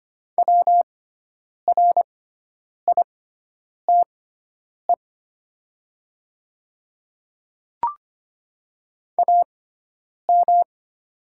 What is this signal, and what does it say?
Morse code, heard as a keyed tone, about 25 wpm, spaced as for 9 wpm, 700 Hz. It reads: WRITE AM
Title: 6x Character Spacing - Top 500 Words - 25wpm